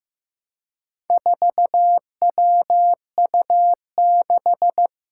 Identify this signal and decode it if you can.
Morse code, heard as a keyed tone, about 15 wpm, 700 Hz. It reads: 4WU6
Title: Four Challenging Letter-Number Combinations 15wpm